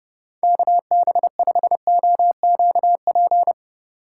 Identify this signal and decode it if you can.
Morse code, heard as a keyed tone, about 30 wpm, 700 Hz. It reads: KB5OQP
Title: USA Call Signs - 3x Character Spacing - 30wpm